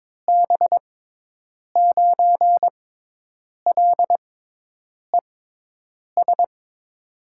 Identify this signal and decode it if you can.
Morse code, heard as a keyed tone, about 22 wpm, spaced as for 8 wpm, 700 Hz. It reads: B9LES
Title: USA Call Signs - 6x Character Spacing - 22wpm